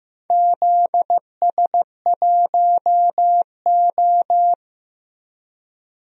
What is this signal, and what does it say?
Morse code, heard as a keyed tone, about 15 wpm, 700 Hz. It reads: ZS1O
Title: DXCC-100 Call Signs - 4 Characters - Repeated 3x - 15wpm